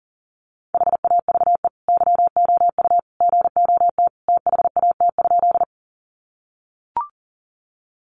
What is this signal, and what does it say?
Morse code, heard as a keyed tone, about 40 wpm, 700 Hz. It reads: HAVE YOU GOT THAT?
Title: Sentences from Top 300 Words 40wpm